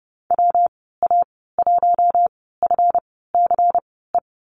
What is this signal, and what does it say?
Morse code, heard as a keyed tone, about 30 wpm, spaced as for 17 wpm, 700 Hz. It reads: WA1FCE
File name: Lesson 37g - Letter x - Call Signs 30wpm